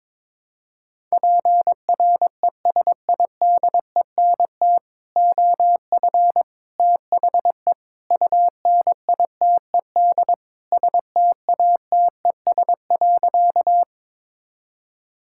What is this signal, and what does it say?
Morse code, heard as a keyed tone, about 22 wpm, 700 Hz. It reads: PRESIDENT OF THE UNITED STATES.
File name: practice / 22wpm / Sentences with 4-Word Phrases - 22wpm